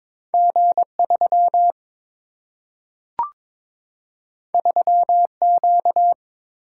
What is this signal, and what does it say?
Morse code, heard as a keyed tone, about 22 wpm, 700 Hz. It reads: G 3 3 Q
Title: Single-Letter Single-Number 22wpm